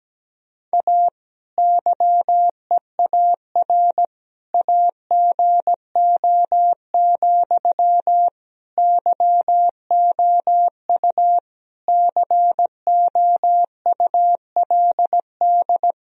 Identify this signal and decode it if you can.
Morse code, heard as a keyed tone, about 17 wpm, 700 Hz. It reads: A YEAR AGO, YOU COULD
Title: Sentences from Top 1000 Words - Review Part 3 - 17wpm